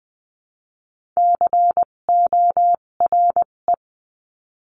Morse code, heard as a keyed tone, about 20 wpm, spaced as for 17 wpm, 700 Hz. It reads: CORE
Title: 257 Lesson 42b - Words 14-20wpm